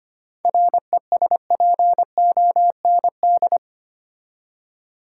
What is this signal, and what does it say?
Morse code, heard as a keyed tone, about 25 wpm, 700 Hz. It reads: RESPOND
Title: Seven Letter Words - 25wpm